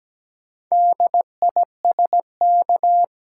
Morse code, heard as a keyed tone, about 17 wpm, 700 Hz. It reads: DISK
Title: Sentences from Top 1000 Words - Review Part 6 - New Word Drill - 17wpm